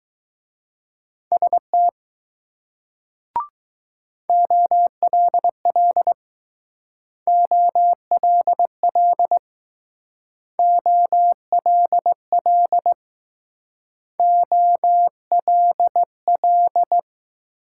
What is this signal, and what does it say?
Morse code, heard as a keyed tone, about 19 wpm, 700 Hz. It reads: ST OLL OLL OLL OLL